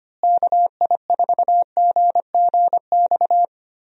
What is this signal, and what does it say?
Morse code, heard as a keyed tone, about 25 wpm, 700 Hz. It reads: KI4GGX
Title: USA Call Signs - 6 Characters - Repeated 3x - 25wpm